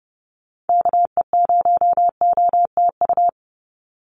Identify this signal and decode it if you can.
Morse code, heard as a keyed tone, about 28 wpm, 700 Hz. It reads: KE0OTU
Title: USA Call Signs 30wpm